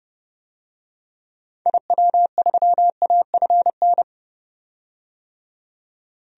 Morse code, heard as a keyed tone, about 30 wpm, 700 Hz. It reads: IW3AFN
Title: DXCC-100 Call Signs - 6 Characters - 30wpm